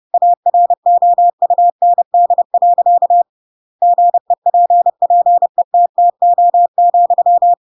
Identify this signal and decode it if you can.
Morse code, heard as a keyed tone, about 30 wpm, 700 Hz. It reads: AROUND. GEPPETTO,
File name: Pinocchio / Pinocchio ch7 30 30 ik1hge